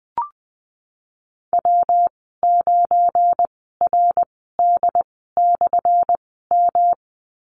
Morse code, heard as a keyed tone, about 20 wpm, spaced as for 15 wpm, 700 Hz. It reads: W9RD/M
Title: 179 Lesson 29g - Slash - Call Signs 20wpm